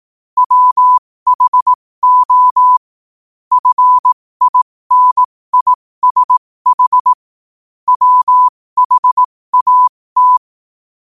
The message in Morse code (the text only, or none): WHO FINISH WHAT